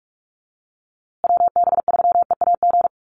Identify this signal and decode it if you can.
Morse code, heard as a keyed tone, about 44 wpm, 700 Hz. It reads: WB3EAG
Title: USA Call Signs Repeated 3x 45wpm